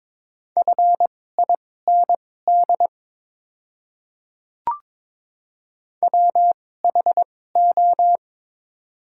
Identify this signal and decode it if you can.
Morse code, heard as a keyed tone, about 23 wpm, spaced as for 16 wpm, 690 Hz. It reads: FIND WHO
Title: 2x Character Spacing - Top 300 Words - 22wpm